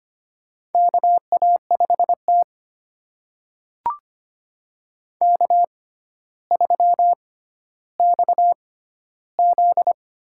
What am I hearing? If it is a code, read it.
Morse code, heard as a keyed tone, about 25 wpm, spaced as for 13 wpm, 700 Hz. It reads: KA5T K3XZ
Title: USA Call Signs - 4 Characters - 6x Character Spacing - 25wpm